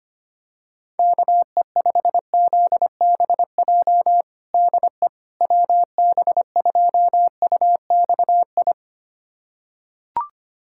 Morse code, heard as a keyed tone, about 25 wpm, 700 Hz. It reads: KE5ZBJ DE WB2UXI
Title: USA Callsign DE Callsign 25wpm